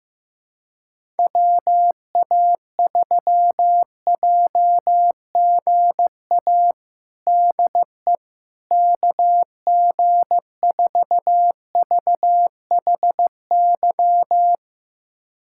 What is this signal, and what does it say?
Morse code, heard as a keyed tone, about 15 wpm, 700 Hz. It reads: WA3JGA DE KG4VHY
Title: USA Callsign DE Callsign 15wpm